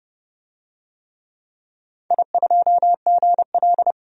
Morse code, heard as a keyed tone, about 30 wpm, 700 Hz. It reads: I2GL
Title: DXCC-100 Call Signs - 4 Characters - 30wpm